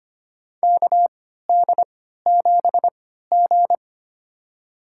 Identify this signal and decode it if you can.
Morse code, heard as a keyed tone, about 25 wpm, spaced as for 14 wpm, 700 Hz. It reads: KD7G
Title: USA Call Signs - 4 Characters - 3x Character Spacing - 25wpm